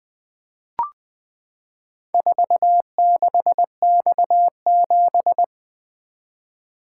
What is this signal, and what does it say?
Morse code, heard as a keyed tone, about 20 wpm, 700 Hz. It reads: 46X7
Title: Four Letter-Number Combinations - 20wpm